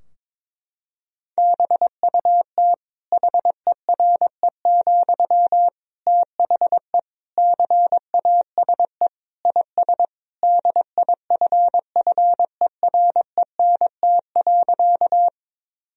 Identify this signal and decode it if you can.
Morse code, heard as a keyed tone, about 22 wpm, 700 Hz. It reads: BUT HERE, THE CASE IS DIFFERENT.
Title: Sentences from Top 1000 Words - Review Part 3 - 22wpm